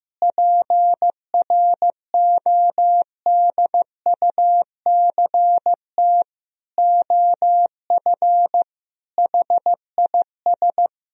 Morse code, heard as a keyed tone, about 15 wpm, 700 Hz. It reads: PRODUCT OF HIS
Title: Sentences from Top 1000 Words - Review Part 2 - 15wpm